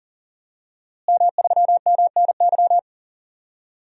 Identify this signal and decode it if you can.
Morse code, heard as a keyed tone, about 40 wpm, 700 Hz. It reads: M3MNY
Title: DXCC-100 Call Signs - 5 Characters - Repeated 3x - 40wpm